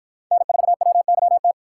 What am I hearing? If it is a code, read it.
Morse code, heard as a keyed tone, about 51 wpm, 700 Hz. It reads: N4WYT